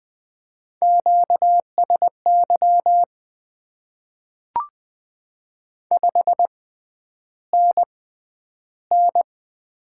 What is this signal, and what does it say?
Morse code, heard as a keyed tone, about 20 wpm, spaced as for 10 wpm, 700 Hz. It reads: QSY 5NN
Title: Top 90 QSO Elements - 6x Character Spacing - 20wpm